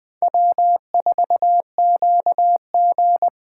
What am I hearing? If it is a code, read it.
Morse code, heard as a keyed tone, about 20 wpm, 700 Hz. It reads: W4QG
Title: Speed Racing - USA 1x2 Call Signs 17wpm